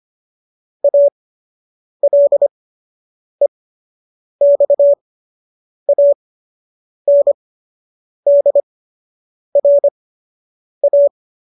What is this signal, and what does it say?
Morse code, heard as a keyed tone, about 25 wpm, spaced as for 8 wpm, 570 Hz. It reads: ALEXANDRA